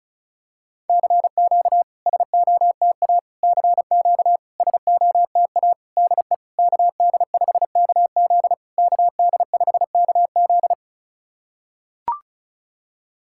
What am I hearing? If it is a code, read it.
Morse code, heard as a keyed tone, about 36 wpm, 690 Hz. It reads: CQ SOTA CQ SOTA DE KD5KZ KD5KZ
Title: SOTA QSO Practice 35wpm